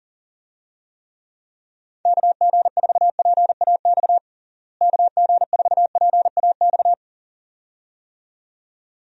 Morse code, heard as a keyed tone, about 40 wpm, 700 Hz. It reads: KG4PAX KG4PAX
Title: USA Call Signs - 6 Characters - Repeated 2x - 40wpm